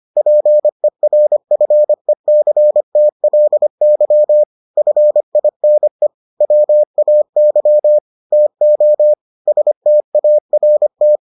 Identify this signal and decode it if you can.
Morse code, heard as a keyed tone, about 25 wpm, 600 Hz. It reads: PERFECTLY FINE WAY TO START